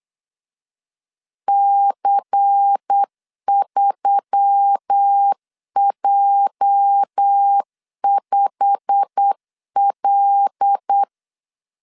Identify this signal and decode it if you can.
Morse code, heard as a keyed tone, about 8 wpm, 800 Hz. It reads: C3J5L
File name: CW / DET / 1 9 1500C S B 8wpm 5624